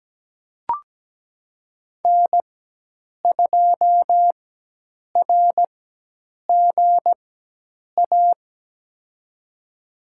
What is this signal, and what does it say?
Morse code, heard as a keyed tone, about 17 wpm, spaced as for 8 wpm, 700 Hz. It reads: N2RGA
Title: USA Call Signs - 5 Characters - 4x Character Spacing - 17wpm